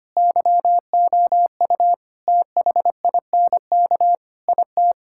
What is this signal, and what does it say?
Morse code, heard as a keyed tone, about 25 wpm, 700 Hz. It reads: YOU THINK IT
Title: Sentences from Top 1000 Words - Review Part 6 - 25wpm